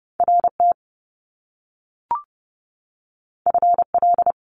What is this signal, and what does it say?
Morse code, heard as a keyed tone, about 30 wpm, 700 Hz. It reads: RT FL
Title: Top Two Letter Combinations in Words 30wpm